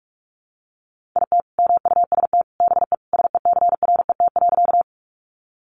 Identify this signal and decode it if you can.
Morse code, heard as a keyed tone, about 44 wpm, 700 Hz. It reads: IT MUST BE SECRET.
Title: Sentences from Top 1200 Words 45wpm